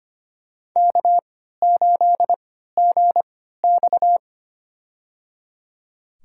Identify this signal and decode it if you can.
Morse code, heard as a keyed tone, about 25 wpm, spaced as for 14 wpm, 700 Hz. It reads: K8GX
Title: USA Call Signs - 4 Characters - 3x Character Spacing - 25wpm